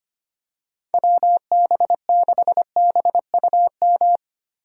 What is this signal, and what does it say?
Morse code, heard as a keyed tone, about 25 wpm, 700 Hz. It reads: WB6BUM
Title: USA Call Signs - 6 Characters - Repeated 2x - 25wpm